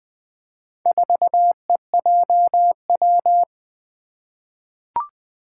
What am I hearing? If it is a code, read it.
Morse code, heard as a keyed tone, about 20 wpm, 700 Hz. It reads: 4EJW
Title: Four Challenging Letter-Number Combinations 20wpm